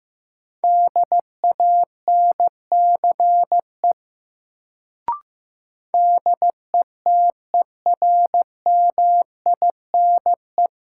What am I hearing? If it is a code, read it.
Morse code, heard as a keyed tone, about 16 wpm, 700 Hz. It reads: DANCE DETERMINE
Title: Sentences from Top 700 Words - New Word Drill - 15wpm